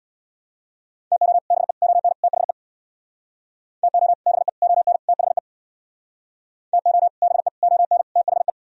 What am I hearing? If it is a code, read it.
Morse code, heard as a keyed tone, about 64 wpm, 700 Hz. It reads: TO BE ON THE TO BE ON THE TO BE ON THE